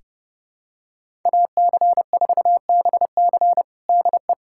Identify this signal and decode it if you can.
Morse code, heard as a keyed tone, about 32 wpm, 700 Hz. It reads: AC4BC DE
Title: USA Callsign DE Callsign - 5 Characters - 30wpm